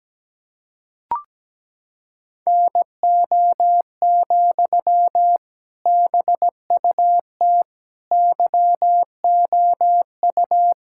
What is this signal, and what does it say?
Morse code, heard as a keyed tone, about 17 wpm, 700 Hz. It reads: NO, BUT YOU
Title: Sentences from Top 1000 Words - Review Part 6 - 17wpm